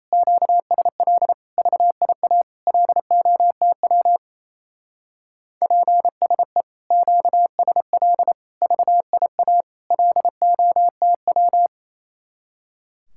Morse code, heard as a keyed tone, about 36 wpm, 700 Hz. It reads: QSL VIA LOTW PSE QSL VIA LOTW